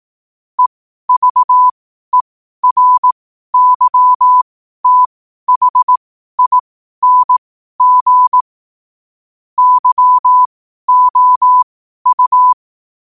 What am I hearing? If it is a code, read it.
Morse code, heard as a keyed tone, about 18 wpm, spaced as for 12 wpm, 1000 Hz. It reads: EVERYTHING YOU